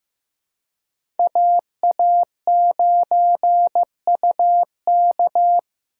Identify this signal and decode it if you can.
Morse code, heard as a keyed tone, about 15 wpm, 700 Hz. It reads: AA9UK
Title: USA Call Signs - 5 Characters - 2x Character Spacing - 15wpm